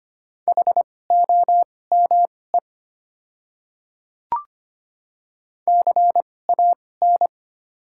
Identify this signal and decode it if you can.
Morse code, heard as a keyed tone, about 25 wpm, spaced as for 18 wpm, 690 Hz. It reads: HOME CAN
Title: 2x Character Spacing - Top 200 Words - 25wpm